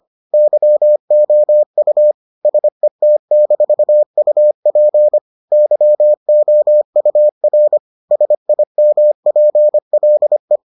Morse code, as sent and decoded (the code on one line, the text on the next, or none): -.-- --- ..- / ... - -....- ..- .--. / -.-- --- ..- .-. / ... .. -- .--. .-.. .
YOU SET-UP YOUR SIMPLE